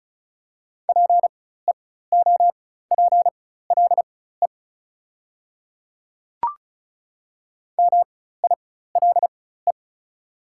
Morse code, heard as a keyed tone, about 34 wpm, spaced as for 16 wpm, 690 Hz. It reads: PEOPLE MILE
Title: 4x Character Spacing - Top 300 Words - 35wpm